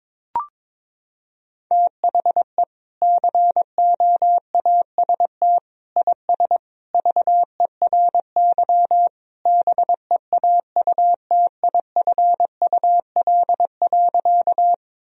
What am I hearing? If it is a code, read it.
Morse code, heard as a keyed tone, about 22 wpm, 700 Hz. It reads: THE COAST IS VERY BEAUTIFUL.
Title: Sentences from Top 1000 Words - Review Part 4 - 22wpm